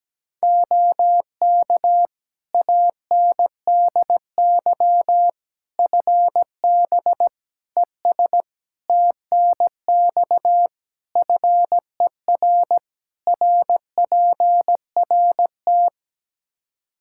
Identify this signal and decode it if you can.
Morse code, heard as a keyed tone, about 17 wpm, 700 Hz. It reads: OK ANDY FB ES TNX FER RPRT
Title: Practice QSO - Exchange 2 - Info - 17wpm